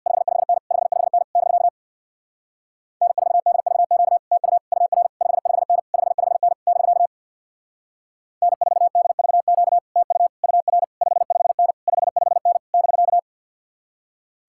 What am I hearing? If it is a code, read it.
Morse code, heard as a keyed tone, about 64 wpm, 700 Hz. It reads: 55N 55N <BK> N4DVQ TU UR 55N 55N <BK> N4DVQ TU UR 55N 55N <BK>